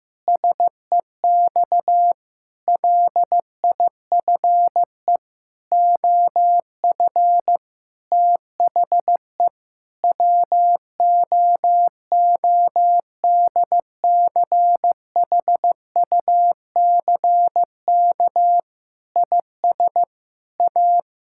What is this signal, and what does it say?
Morse code, heard as a keyed tone, about 15 wpm, 700 Hz. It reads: SEX LIFE OF THE WOODCHUCK IS A
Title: Pangram Challenge v2 - 15wpm